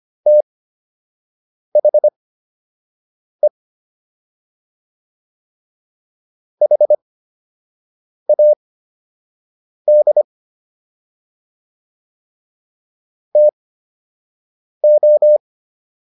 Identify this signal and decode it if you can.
Morse code, heard as a keyed tone, about 25 wpm, spaced as for 6 wpm, 600 Hz. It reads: THE HAD TO